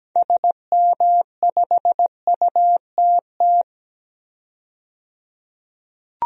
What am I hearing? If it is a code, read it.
Morse code, heard as a keyed tone, about 17 wpm, 700 Hz. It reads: SM5UTT
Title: DXCC-100 Call Signs - 6 Characters - 17wpm